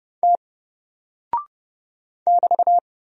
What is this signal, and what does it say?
Morse code, heard as a keyed tone, about 30 wpm, 700 Hz. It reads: T <BT>
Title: Single Character (Letter:Number:Punctuation:Prosign) 30wpm